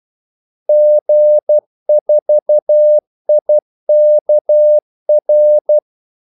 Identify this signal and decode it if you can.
Morse code, heard as a keyed tone, about 12 wpm, 600 Hz. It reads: G4IKR
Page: 2 ekstra ordmellemrum mellem kaldesignalerne